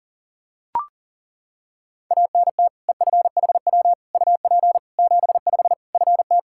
Morse code, heard as a keyed tone, about 37 wpm, 700 Hz. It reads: ANT EFHW UP 75 FT
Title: Practice QSO - Exchange 2 - Info - 40wpm